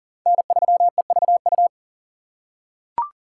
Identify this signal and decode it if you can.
Morse code, heard as a keyed tone, about 40 wpm, 690 Hz. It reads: N3EVU
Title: USA Call Signs - 5 Characters - 6x Character Spacing - 40wpm